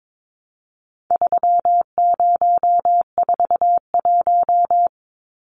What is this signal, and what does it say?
Morse code, heard as a keyed tone, about 22 wpm, 700 Hz. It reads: 3041
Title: Numbers - 4-Digit 22wpm